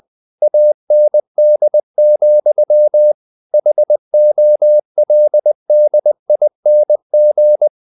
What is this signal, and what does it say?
Morse code, heard as a keyed tone, about 20 wpm, 600 Hz. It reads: AND, HOLDING
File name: the old man and the sea / the old man and the sea part 1 20wpm